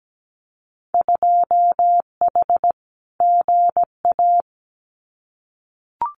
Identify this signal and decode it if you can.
Morse code, heard as a keyed tone, about 17 wpm, 700 Hz. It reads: 2H GA
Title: Winter Field Day Exchanges 17wpm